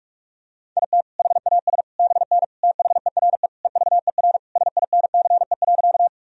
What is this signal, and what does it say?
Morse code, heard as a keyed tone, about 46 wpm, 700 Hz. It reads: IT HAS BN THERE EVER SINCE.
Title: Sentences from Top 400 Words and Abbreviations 45wpm